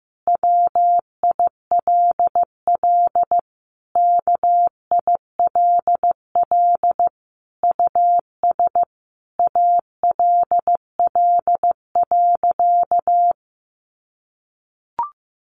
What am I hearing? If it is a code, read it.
Morse code, heard as a keyed tone, about 15 wpm, 700 Hz. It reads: WILL KILL US ALL.
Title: Sentences from Top 1400 Words 15wpm